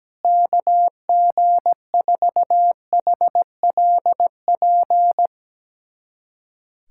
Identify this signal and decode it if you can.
Morse code, heard as a keyed tone, about 17 wpm, 700 Hz. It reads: KG4HLP
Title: USA Callsign DE Callsign - 6 Characters - 17wpm